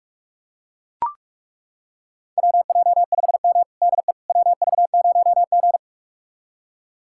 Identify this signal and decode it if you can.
Morse code, heard as a keyed tone, about 41 wpm, 700 Hz. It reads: WJ5M DE WV0G
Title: USA Callsign DE Callsign - 4 Characters - 45wpm